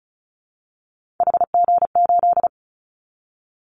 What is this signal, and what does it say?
Morse code, heard as a keyed tone, about 36 wpm, 690 Hz. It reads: HG8
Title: Callsign Prefixes for Top 50 DXCC - 35wpm